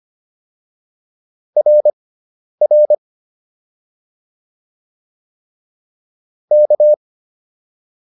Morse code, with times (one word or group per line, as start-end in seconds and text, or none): 1.6-2.9 RR
6.5-6.9 K